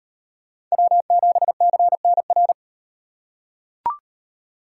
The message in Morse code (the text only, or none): W7CNR